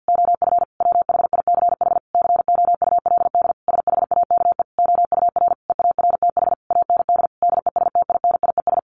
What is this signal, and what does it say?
Morse code, heard as a keyed tone, about 50 wpm, 700 Hz. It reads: OF WHICH COULD SHAKE OUR EARTH AND DESTINIES
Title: A Sound Of Thunder 50wpm